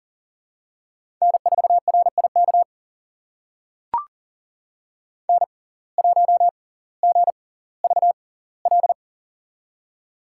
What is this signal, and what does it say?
Morse code, heard as a keyed tone, about 38 wpm, spaced as for 20 wpm, 700 Hz. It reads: N4RIK N1GVL
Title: USA Call Signs - 5 Characters - 6x Character Spacing - 40wpm